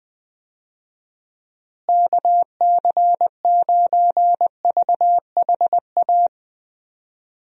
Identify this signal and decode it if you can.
Morse code, heard as a keyed tone, about 21 wpm, 700 Hz. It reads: KC9VHA
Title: USA Call Signs - 6 Characters - Repeated 3x - 20wpm